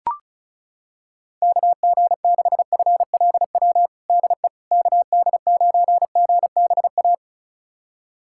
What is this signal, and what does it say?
Morse code, heard as a keyed tone, about 34 wpm, 690 Hz. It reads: KG6FLW DE KD9GBA
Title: USA Callsign DE Callsign 35wpm